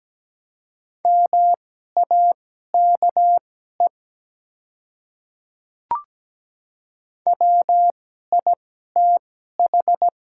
Morse code, heard as a keyed tone, about 17 wpm, spaced as for 12 wpm, 700 Hz. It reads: MAKE WITH
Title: 2x Character Spacing - Top 100 Words - 17wpm